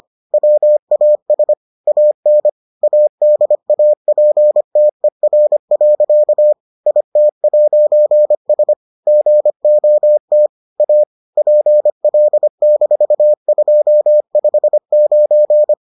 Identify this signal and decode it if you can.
Morse code, heard as a keyed tone, about 25 wpm, 600 Hz. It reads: WAS AN ADAPTER. IT'S GOT A PL-259